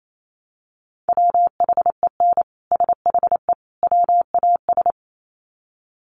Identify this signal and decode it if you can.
Morse code, heard as a keyed tone, about 28 wpm, 700 Hz. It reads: WHEN SHE WAS
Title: Sets of 3 Words 28wpm